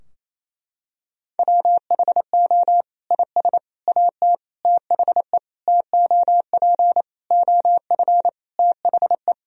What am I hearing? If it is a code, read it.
Morse code, heard as a keyed tone, about 28 wpm, 700 Hz. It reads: WHO IS AT THE TOP OF THE
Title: Sentences from Top 1000 Words - Review Part 4 - 28wpm